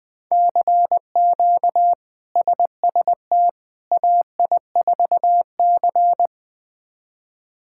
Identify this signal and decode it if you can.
Morse code, heard as a keyed tone, about 20 wpm, 700 Hz. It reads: CQ SST AI4C
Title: K1USN SST QSO Practice - 20wpm